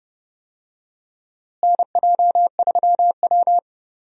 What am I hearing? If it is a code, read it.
Morse code, heard as a keyed tone, about 31 wpm, 700 Hz. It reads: NJ3W
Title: USA Call Signs - 4 Characters - Repeated 2x - 30wpm